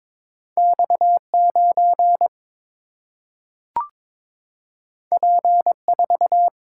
Morse code, heard as a keyed tone, about 22 wpm, 700 Hz. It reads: X 9 P 4
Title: Single-Letter Single-Number 22wpm